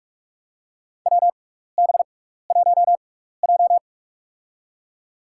Morse code, heard as a keyed tone, about 46 wpm, spaced as for 16 wpm, 700 Hz. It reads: WB1J
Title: USA Call Signs - 4 Characters - 6x Character Spacing - 45wpm